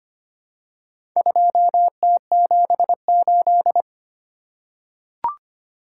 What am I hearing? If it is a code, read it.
Morse code, heard as a keyed tone, about 25 wpm, 700 Hz. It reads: 2T78
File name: practice / 25wpm / Four Letter-Number Combinations - 25wpm